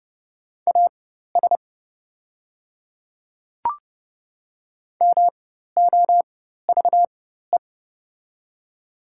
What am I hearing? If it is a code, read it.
Morse code, heard as a keyed tone, about 32 wpm, spaced as for 14 wpm, 700 Hz. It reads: AS MOVE
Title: 4x Character Spacing - Top 200 Words - 30wpm